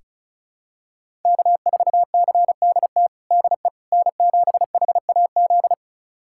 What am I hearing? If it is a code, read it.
Morse code, heard as a keyed tone, about 36 wpm, 700 Hz. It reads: K4CDT DE N7HAZ
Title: USA Callsign DE Callsign - 5 Characters - 35wpm